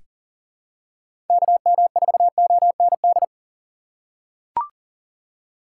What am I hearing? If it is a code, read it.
Morse code, heard as a keyed tone, about 40 wpm, 700 Hz. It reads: KM4OND
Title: USA Call Signs - 6 Characters - 3x Character Spacing - 40wpm